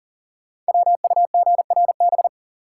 Morse code, heard as a keyed tone, about 42 wpm, 700 Hz. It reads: WUGRB
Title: Code Groups - 5 Letters - 40wpm